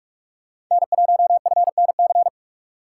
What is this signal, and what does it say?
Morse code, heard as a keyed tone, about 43 wpm, 700 Hz. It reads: N1FNC
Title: USA Call Signs - 5 Characters - 3x Character Spacing - 45wpm